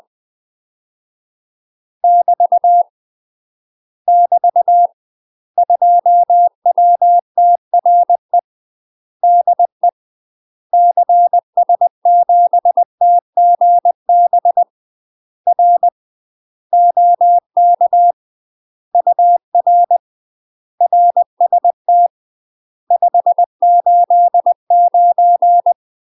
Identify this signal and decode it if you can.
Morse code, heard as a keyed tone, about 20 wpm, 700 Hz. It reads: <BT> <BT> 2WTRE DE CS7TGB R OK UR RST 589
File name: UK MASTER QSOs TRAINING / 20WPM / UK MASTER QSOs 0070